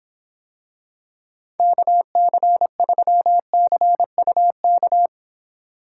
Morse code, heard as a keyed tone, about 26 wpm, 700 Hz. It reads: KC3CUK